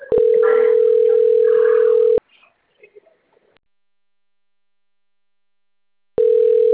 Ringback tone (North America), 440 + 480 Hz.